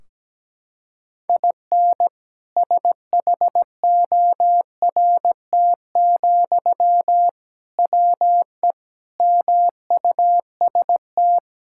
Morse code, heard as a keyed tone, about 17 wpm, 700 Hz. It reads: IN SHORT, WE MUST
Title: Sentences from Top 1600 Words 17wpm